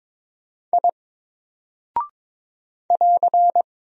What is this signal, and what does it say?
Morse code, heard as a keyed tone, about 21 wpm, 700 Hz. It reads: I <AR>
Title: Single Character (Letter:Number:Punctuation:Prosign) Alt 22wpm